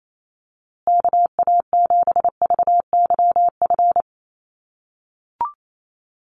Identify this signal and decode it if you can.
Morse code, heard as a keyed tone, about 28 wpm, 700 Hz. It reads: KA7VYF
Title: USA Call Signs Repeated 3x 28wpm